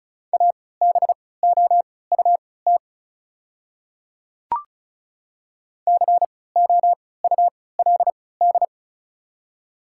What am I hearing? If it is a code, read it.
Morse code, heard as a keyed tone, about 34 wpm, spaced as for 20 wpm, 700 Hz. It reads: ABOUT COULD
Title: 3x Character Spacing - Top 100 Words - 35wpm